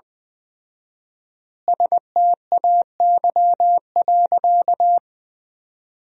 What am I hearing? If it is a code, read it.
Morse code, heard as a keyed tone, about 20 wpm, 700 Hz. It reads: STAY.
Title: 4x Word-Spacing - Sentences from Top 500 Words - 20wpm